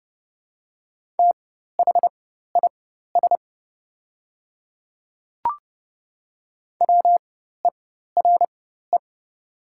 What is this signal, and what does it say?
Morse code, heard as a keyed tone, about 29 wpm, spaced as for 14 wpm, 700 Hz. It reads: THIS WERE
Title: 4x Character Spacing - Top 100 Words - 30wpm